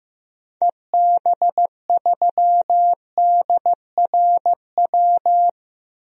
Morse code, heard as a keyed tone, about 15 wpm, 700 Hz. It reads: EB3DRW
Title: DXCC-100 Call Signs - 6 Characters - 15wpm